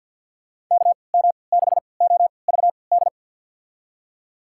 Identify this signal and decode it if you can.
Morse code, heard as a keyed tone, about 47 wpm, spaced as for 28 wpm, 700 Hz. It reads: KM6OVD
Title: USA Call Signs - 3x Character Spacing - 50wpm